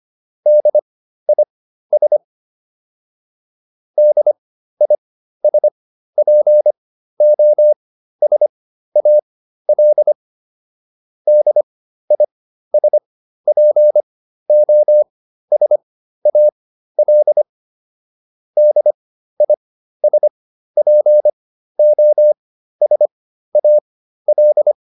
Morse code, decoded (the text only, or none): DIS DISPOSAL DISPOSAL DISPOSAL